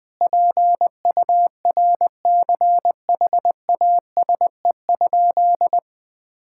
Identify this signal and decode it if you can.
Morse code, heard as a keyed tone, about 20 wpm, 700 Hz. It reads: PURCHASE?
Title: Sentences from Top 1000 Words - Review Part 5 - 20wpm